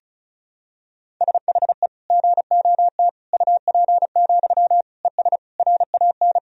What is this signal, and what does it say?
Morse code, heard as a keyed tone, about 36 wpm, 700 Hz. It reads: SHE GOT UP, ES RAN
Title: Sentences from Top 500 Words and Abbreviations 35wpm